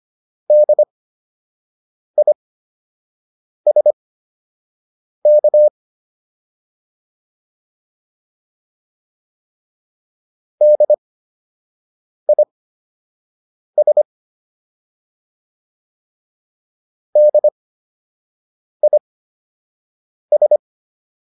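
Morse code, heard as a keyed tone, about 25 wpm, spaced as for 6 wpm, 600 Hz. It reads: DISK DIS DIS
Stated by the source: Fundamental class Morse code practice material.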